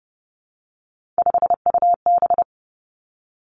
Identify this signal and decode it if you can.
Morse code, heard as a keyed tone, about 30 wpm, 700 Hz. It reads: 5UB
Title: Three Challenging Letter-Number Combinations 30wpm